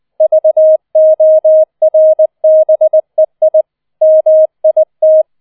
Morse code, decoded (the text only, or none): VORBEI MIT